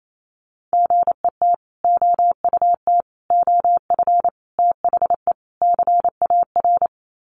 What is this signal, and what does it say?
Morse code, heard as a keyed tone, about 28 wpm, 700 Hz. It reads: GET OUT OF THE CAR
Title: Sets of 5 Words Encore2 28wpm